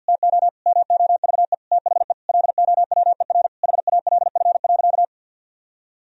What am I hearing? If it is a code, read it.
Morse code, heard as a keyed tone, about 50 wpm, 700 Hz. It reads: TO MOVE THE LOWER HALF.
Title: Sentences from Top 1000 Words - Review Part 4 - 50wpm